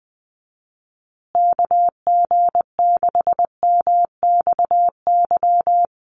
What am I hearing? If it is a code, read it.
Morse code, heard as a keyed tone, about 20 wpm, 700 Hz. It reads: KG6MXY
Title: USA Call Signs Repeated 3x 20wpm